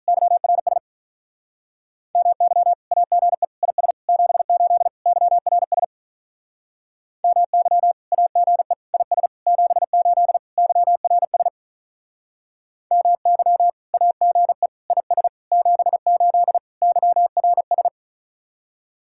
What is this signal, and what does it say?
Morse code, heard as a keyed tone, about 42 wpm, 700 Hz. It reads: YRS MY AGE IS 78 YRS MY AGE IS 78 YRS MY AGE IS 78 YRS
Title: Speed-Racing Common QSO Phrases - 35wpm